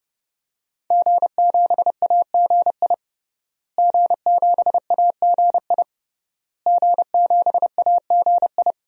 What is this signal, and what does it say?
Morse code, heard as a keyed tone, about 30 wpm, 700 Hz. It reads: G7AGI G7AGI G7AGI
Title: DXCC-100 Call Signs - 5 Characters - Repeated 3x - 30wpm